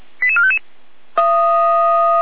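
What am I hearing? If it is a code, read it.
Five-tone selective call (ZVEI), heard as quick runs of five tones, tones of about 70 ms. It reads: 88455
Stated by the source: ZVEI five-tone selective call received by SDR.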